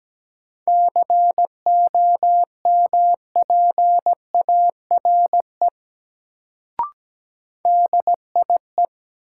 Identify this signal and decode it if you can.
Morse code, heard as a keyed tone, about 17 wpm, 700 Hz. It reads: COMPARE DIE
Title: Sentences from Top 1300 Words - New Word Drill - 17wpm